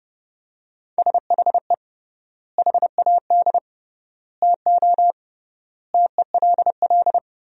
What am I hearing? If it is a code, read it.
Morse code, heard as a keyed tone, about 30 wpm, 700 Hz. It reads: SHE HAD TO TELL
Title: 3x Word-Spacing - Sentences from Top 200 Words - 30wpm